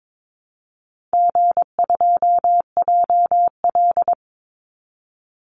Morse code, heard as a keyed tone, about 22 wpm, 700 Hz. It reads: G2JL
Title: Easy Call Signs Repeated 3x - 22wpm